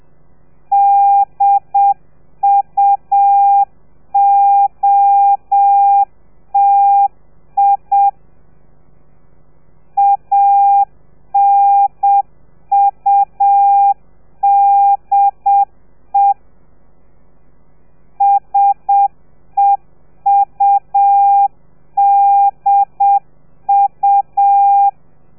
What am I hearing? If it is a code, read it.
Morse code, heard as a keyed tone, about 7 wpm, 800 Hz. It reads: DUOTI ANUDE SEUDU